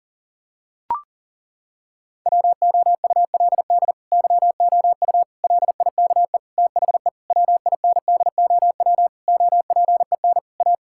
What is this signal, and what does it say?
Morse code, heard as a keyed tone, about 37 wpm, 700 Hz. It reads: WOULD YOU LIKE THE WINDOW OPEN A
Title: Sentences from Top 1600 Words 40wpm